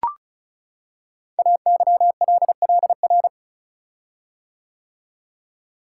Morse code, heard as a keyed tone, about 36 wpm, 700 Hz. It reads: AYLLR
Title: Code Groups - 5 Letters - 35wpm